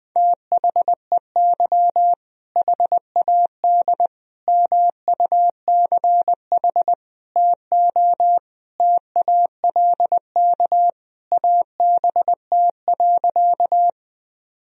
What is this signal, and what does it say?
Morse code, heard as a keyed tone, about 20 wpm, 700 Hz. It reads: THEY HAD MUCH TO TALK ABT.
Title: Sentences from Top 400 Words and Abbreviations 20wpm